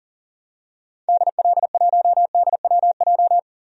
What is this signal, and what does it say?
Morse code, heard as a keyed tone, about 40 wpm, 700 Hz. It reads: DL1DWJ
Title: DXCC-100 Call Signs - 6 Characters - 40wpm